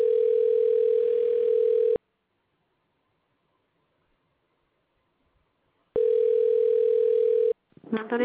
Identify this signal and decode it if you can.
Ringback tone (North America), 440 + 480 Hz, with speech.
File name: recordings / RE738d7a42899ce94609b4813d78ef2829